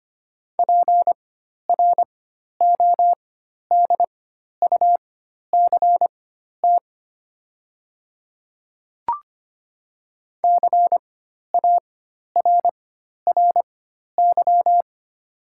Morse code, heard as a keyed tone, about 25 wpm, spaced as for 12 wpm, 700 Hz. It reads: PRODUCT CARRY
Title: 4x Character Spacing - Top 500 Words - 25wpm